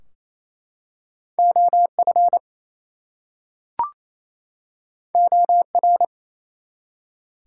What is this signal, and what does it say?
Morse code, heard as a keyed tone, about 28 wpm, 700 Hz. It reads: OF OR
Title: Two Letter Words - 28wpm